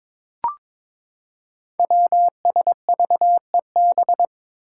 Morse code, heard as a keyed tone, about 21 wpm, 700 Hz. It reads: WSVEB
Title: Five Challenging Letter-Number Combinations 22wpm